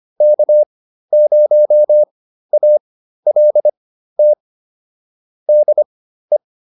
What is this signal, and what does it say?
Morse code, heard as a keyed tone, about 25 wpm, spaced as for 13 wpm, 600 Hz. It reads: K0ALT DE
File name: QSO202 13